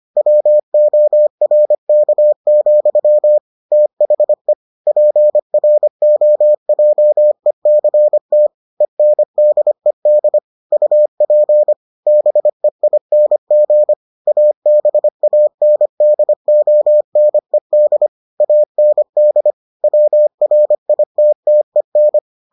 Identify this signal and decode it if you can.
Morse code, heard as a keyed tone, about 25 wpm, 600 Hz. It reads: WORK, THE PROJECT ENDED UP BEING ABANDONED AND WRITTEN